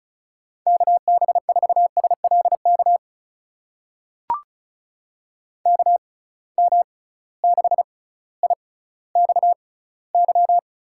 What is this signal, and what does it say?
Morse code, heard as a keyed tone, about 34 wpm, spaced as for 18 wpm, 700 Hz. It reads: KB4SLK KM6IXY
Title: USA Call Signs - 6 Characters - 6x Character Spacing - 35wpm